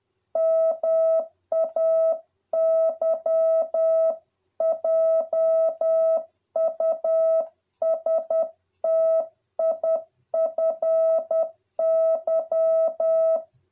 Morse code, heard as a keyed tone, about 10 wpm, 650 Hz. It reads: MAY JUSTIFY